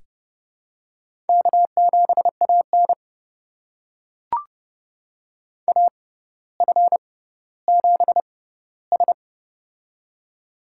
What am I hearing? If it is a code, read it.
Morse code, heard as a keyed tone, about 30 wpm, spaced as for 15 wpm, 700 Hz. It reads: K7AN AF7S
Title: USA Call Signs - 4 Characters - 6x Character Spacing - 30wpm